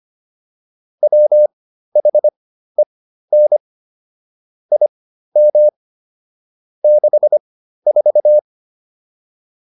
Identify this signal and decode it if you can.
Morse code, heard as a keyed tone, about 25 wpm, spaced as for 13 wpm, 600 Hz. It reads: WHEN IM 64